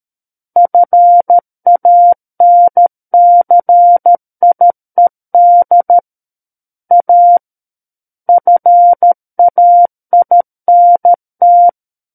Morse code, heard as a keyed tone, about 13 wpm, 700 Hz. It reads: FANCIED A FAINT
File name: Book One / 04-Chapter 4 - 13 WPM